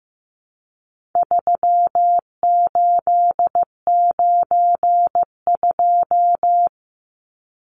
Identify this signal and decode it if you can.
Morse code, heard as a keyed tone, about 15 wpm, 700 Hz. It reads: 3892
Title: Numbers - 4-Digit 15wpm